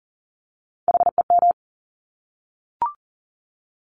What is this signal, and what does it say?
Morse code, heard as a keyed tone, about 40 wpm, 700 Hz. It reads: HEM
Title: Top Three Letter Combinations in Words 40wpm